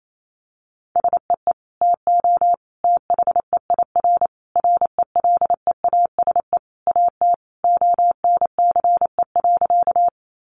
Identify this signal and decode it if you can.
Morse code, heard as a keyed tone, about 28 wpm, 700 Hz. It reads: SEE TO THEIR RELEASE AT ONCE.
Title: Sentences from Top 1100 Words 28wpm